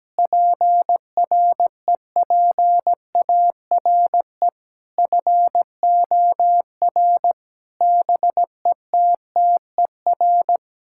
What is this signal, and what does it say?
Morse code, heard as a keyed tone, about 17 wpm, 700 Hz. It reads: PREPARE FOR BETTER
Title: Sentences from Top 1000 Words - Review Part 6 - 17wpm